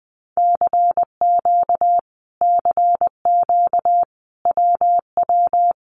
Morse code, heard as a keyed tone, about 20 wpm, 700 Hz. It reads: CQ CQ WW
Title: Common QSO Phrases Encore 20wpm